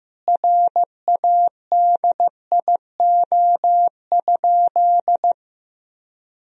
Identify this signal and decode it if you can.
Morse code, heard as a keyed tone, about 15 wpm, 700 Hz. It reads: RADIO?
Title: Sentences from Top 1000 Words - Review Part 3 - 15wpm